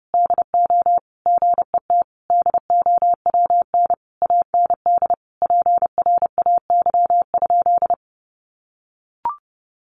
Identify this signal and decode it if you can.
Morse code, heard as a keyed tone, about 30 wpm, 700 Hz. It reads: DO GET DOWN AND PRAY?
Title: A Sound Of Thunder 30wpm